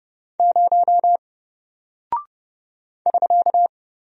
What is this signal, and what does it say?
Morse code, heard as a keyed tone, about 28 wpm, 700 Hz. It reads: 0 <SK>
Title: Single Character (Letter:Number:Punctuation:Prosign) 30wpm